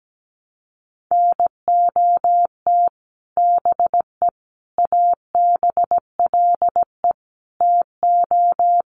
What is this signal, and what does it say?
Morse code, heard as a keyed tone, about 17 wpm, 700 Hz. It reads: NOT BE ABLE TO
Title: Sets of 4 Words 17wpm